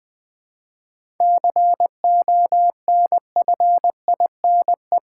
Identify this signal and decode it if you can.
Morse code, heard as a keyed tone, about 20 wpm, 700 Hz. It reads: CONFINE
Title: 7 Letter Words - First 3 Letters Given - 20wpm